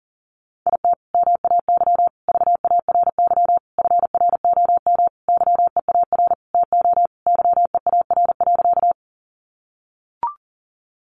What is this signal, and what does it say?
Morse code, heard as a keyed tone, about 40 wpm, 700 Hz. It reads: IT MAY VARY FROM YEAR TO YEAR.
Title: Sentences from Top 1200 Words 40wpm